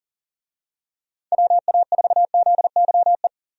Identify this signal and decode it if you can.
Morse code, heard as a keyed tone, about 40 wpm, 700 Hz. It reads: WA4ZYE
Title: USA Call Signs - 6 Characters - 40wpm